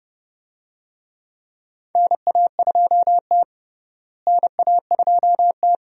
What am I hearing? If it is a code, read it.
Morse code, heard as a keyed tone, about 30 wpm, 700 Hz. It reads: NA2T NA2T
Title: USA Call Signs - 4 Characters - Repeated 2x - 30wpm